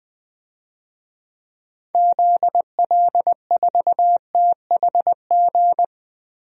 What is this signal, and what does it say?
Morse code, heard as a keyed tone, about 21 wpm, 700 Hz. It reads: ZL4THG
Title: DXCC-100 Call Signs - 6 Characters - 20wpm